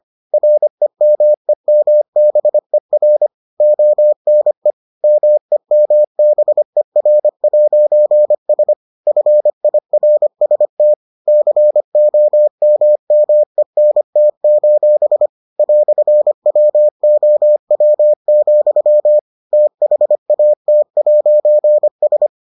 Morse code, heard as a keyed tone, about 25 wpm, 600 Hz. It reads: REMEMBER ONE MEMBER'S FIRST COMMENT: "WOW, THAT'S